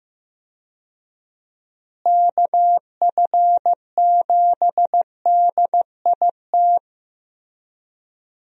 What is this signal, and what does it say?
Morse code, heard as a keyed tone, about 15 wpm, 700 Hz. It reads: KF7DIT
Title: USA Call Signs - 6 Characters - Repeated 3x - 15wpm